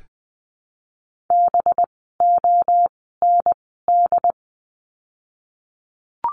Morse code, heard as a keyed tone, about 20 wpm, 700 Hz. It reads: B O N D
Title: 185 Lesson 30f - Letter b - Words 20wpm